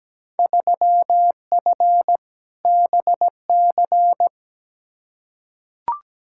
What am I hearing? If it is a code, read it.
Morse code, heard as a keyed tone, about 17 wpm, 700 Hz. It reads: 3F BC
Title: Field Day Exchanges Encore - 17wpm